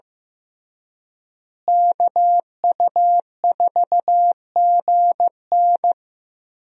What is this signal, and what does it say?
Morse code, heard as a keyed tone, about 15 wpm, 700 Hz. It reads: KU4GN
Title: USA Call Signs - 5 Characters - Repeated 3x - 15wpm